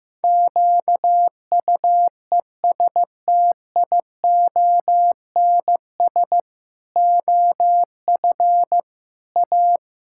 Morse code, heard as a keyed tone, about 15 wpm, 700 Hz. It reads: QUESTIONS OF A
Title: Pangram Challenge v2 - 15wpm